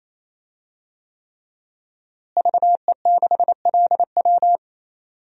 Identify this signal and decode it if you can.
Morse code, heard as a keyed tone, about 28 wpm, 700 Hz. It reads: VE6LW
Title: DXCC-100 Call Signs - 5 Characters - Repeated 3x - 28wpm